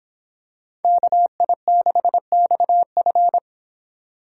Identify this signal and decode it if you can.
Morse code, heard as a keyed tone, about 26 wpm, 700 Hz. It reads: KI6XF